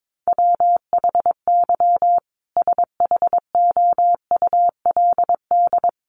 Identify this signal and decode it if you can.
Morse code, heard as a keyed tone, about 22 wpm, 700 Hz. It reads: WHY SHOULD
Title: Sentences from Top 1300 Words 22wpm